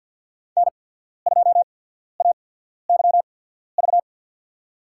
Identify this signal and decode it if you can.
Morse code, heard as a keyed tone, about 54 wpm, spaced as for 14 wpm, 700 Hz. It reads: N2AYV
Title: USA Call Signs - 5 Characters - 8x Character Spacing - 50wpm